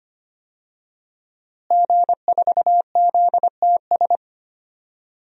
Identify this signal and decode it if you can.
Morse code, heard as a keyed tone, about 25 wpm, 700 Hz. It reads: G4ZTS
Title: DXCC-100 Call Signs - 5 Characters - Repeated 3x - 25wpm